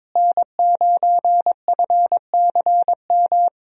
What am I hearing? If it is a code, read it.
Morse code, heard as a keyed tone, about 22 wpm, 700 Hz. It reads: N9FCM
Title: USA Call Signs - 5 Characters - Repeated 2x - 22wpm